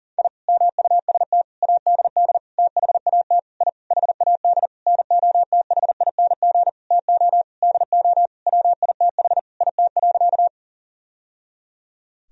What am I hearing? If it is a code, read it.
Morse code, heard as a keyed tone, about 40 wpm, 700 Hz. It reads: I MUST ADD THAT I HAD NOTHING TO DO WITH IT.
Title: Sentences from Top 1000 Words - Review Part 1 - 40wpm